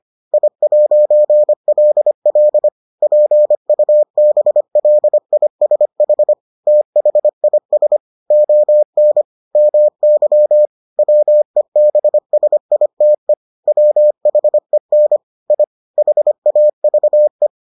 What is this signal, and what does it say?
Morse code, heard as a keyed tone, about 25 wpm, 600 Hz. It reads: I'LL PUBLISH THIS ON MY WEBSITE WHEN I HAVE